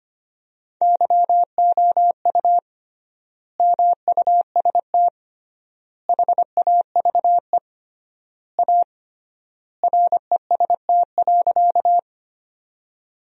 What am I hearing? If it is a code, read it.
Morse code, heard as a keyed tone, about 25 wpm, 700 Hz. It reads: YOU MUST HAVE A REST.
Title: 3x Word-Spacing - Sentences from Top 500 Words - 25wpm